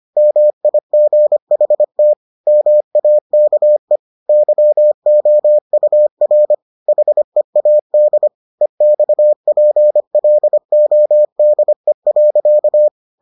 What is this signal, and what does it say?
Morse code, heard as a keyed tone, about 25 wpm, 600 Hz. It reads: MIGHT MAKE YOUR HEAD EXPLODE.